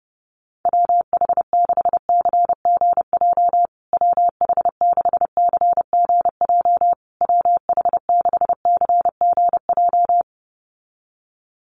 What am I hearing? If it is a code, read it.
Morse code, heard as a keyed tone, about 30 wpm, 700 Hz. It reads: WH6CGJ WH6CGJ WH6CGJ
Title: USA Call Signs Repeated 3x 30wpm